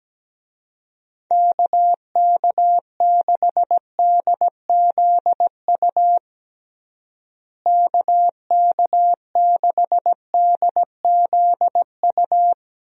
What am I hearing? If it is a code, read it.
Morse code, heard as a keyed tone, about 17 wpm, 700 Hz. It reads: KK6DZU KK6DZU
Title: USA Call Signs - 6 Characters - Repeated 3x - 17wpm